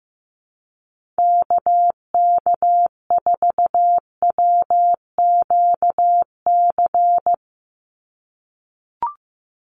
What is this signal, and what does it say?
Morse code, heard as a keyed tone, about 15 wpm, 700 Hz. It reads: KK4WQC
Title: USA Call Signs 15wpm